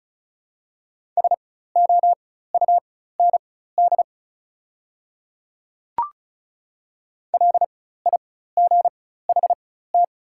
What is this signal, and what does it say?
Morse code, heard as a keyed tone, about 36 wpm, spaced as for 16 wpm, 700 Hz. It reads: SOUND LIGHT
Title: 4x Character Spacing - Top 200 Words - 35wpm